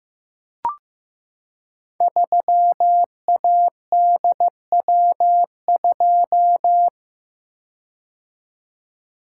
Morse code, heard as a keyed tone, about 15 wpm, 700 Hz. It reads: 3ADW2
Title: Code Groups - 5 Letter:Numbers - 15wpm